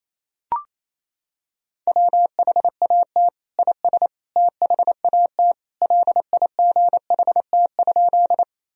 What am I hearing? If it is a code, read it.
Morse code, heard as a keyed tone, about 28 wpm, 700 Hz. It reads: WHAT IS THAT LIGHT?
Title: Sentences from Top 1000 Words - Review Part 1 - 28wpm